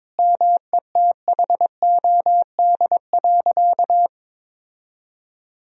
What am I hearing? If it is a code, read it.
Morse code, heard as a keyed tone, about 22 wpm, 700 Hz. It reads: METHOD.
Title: Sentences from Top 1000 Words - Review Part 3 - 22wpm